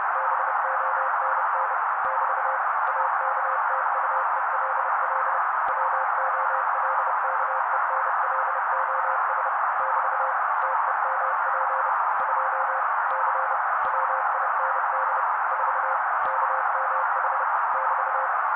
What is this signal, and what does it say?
Morse code, heard as a keyed tone, about 30 wpm, 530 Hz. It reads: BONN <BT> AKTUELL WOLKENLOS <BT> TEMP 2 C WIND 4 KMH <BT>
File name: daily / 29wpm noise